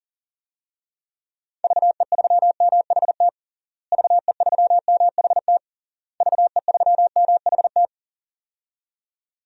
Morse code, heard as a keyed tone, about 40 wpm, 700 Hz. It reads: VE3MHT VE3MHT VE3MHT
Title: DXCC-100 Call Signs - 6 Characters - Repeated 3x - 40wpm